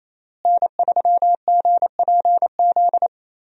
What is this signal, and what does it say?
Morse code, heard as a keyed tone, about 28 wpm, 700 Hz. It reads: N3GPZ
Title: USA Call Signs - 5 Characters - Repeated 2x - 28wpm